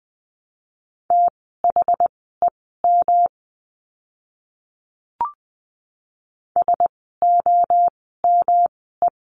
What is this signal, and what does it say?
Morse code, heard as a keyed tone, about 20 wpm, spaced as for 14 wpm, 700 Hz. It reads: THEM SOME
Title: 206 Lesson 33f - Letter j - Words 20wpm